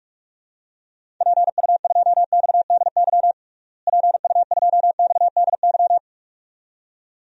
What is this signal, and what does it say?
Morse code, heard as a keyed tone, about 47 wpm, 700 Hz. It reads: PU2XDY PU2XDY
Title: DXCC-100 Call Signs - 6 Characters - Repeated 2x - 45wpm